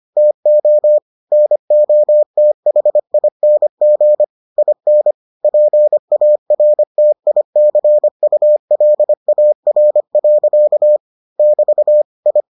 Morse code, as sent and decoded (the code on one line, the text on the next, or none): - --- / -. --- - .... .. -. --. / .. -. / .--. .- .-. - .. -.-. ..- .-.. .- .-. .-.-.- / -...- / ..
TO NOTHING IN PARTICULAR. <BT> I